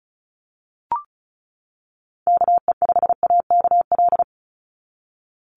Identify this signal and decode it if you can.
Morse code, heard as a keyed tone, about 36 wpm, 700 Hz. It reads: KE5AKL
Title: NA SOTA Activator Callsigns 35wpm